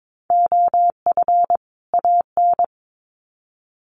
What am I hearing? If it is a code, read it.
Morse code, heard as a keyed tone, about 22 wpm, 700 Hz. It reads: OF AN
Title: Sets of 2 Words 22wpm